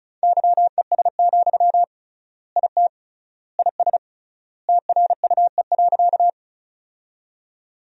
Morse code, heard as a keyed tone, about 36 wpm, 700 Hz. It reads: YES, IT IS TRUE.
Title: 3x Word-Spacing - Sentences from Top 500 Words - 35wpm